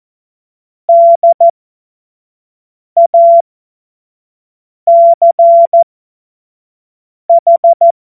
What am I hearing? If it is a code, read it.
Morse code, heard as a keyed tone, about 14 wpm, spaced as for 5 wpm, 680 Hz. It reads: DACH
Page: cw-witze